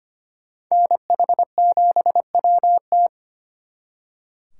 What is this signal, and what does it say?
Morse code, heard as a keyed tone, about 32 wpm, 700 Hz. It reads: NH7WT